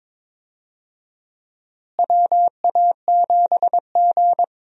Morse code, heard as a keyed tone, about 22 wpm, 700 Hz. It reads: WA7G
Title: USA Call Signs - 4 Characters - Repeated 2x - 22wpm